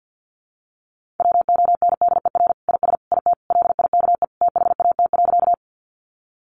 Morse code, heard as a keyed tone, about 50 wpm, 700 Hz. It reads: WONDER IS IT LIKE THAT.
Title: Sentences from Top 500 Words 50wpm